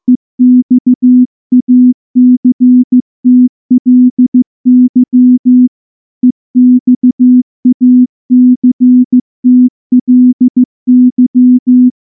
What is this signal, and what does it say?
Morse code, heard as a keyed tone, about 15 wpm, 250 Hz. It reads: EXACTLY EXACTLY
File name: morse / morse-training-15WPM